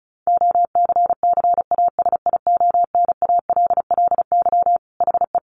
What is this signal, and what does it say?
Morse code, heard as a keyed tone, about 34 wpm, 700 Hz. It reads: OCCASIONALLY HE
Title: Sentences from Top 1300 Words 35wpm